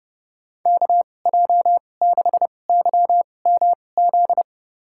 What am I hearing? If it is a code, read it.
Morse code, heard as a keyed tone, about 30 wpm, spaced as for 22 wpm, 700 Hz. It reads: KJ6YMZ
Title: USA Call Signs - 6 Characters - 2x Character Spacing - 30wpm